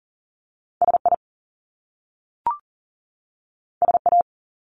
Morse code, heard as a keyed tone, about 38 wpm, 700 Hz. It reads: SI SA
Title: Top Two Letter Combinations in Words 40wpm